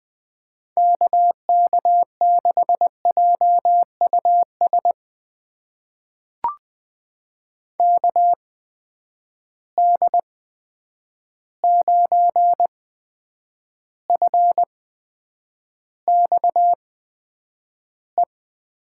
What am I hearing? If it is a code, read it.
Morse code, heard as a keyed tone, about 20 wpm, spaced as for 9 wpm, 700 Hz. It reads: KK6JUS KD9FXE
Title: USA Call Signs - 6 Characters - 8x Character Spacing - 20wpm